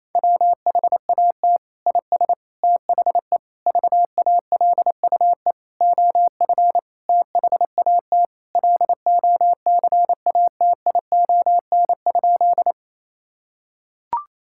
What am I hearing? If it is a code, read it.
Morse code, heard as a keyed tone, about 28 wpm, 700 Hz. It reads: WHAT IS THE VALUE OF THAT LOCATION?
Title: Sentences from Top 1000 Words - Review Part 6 - 28wpm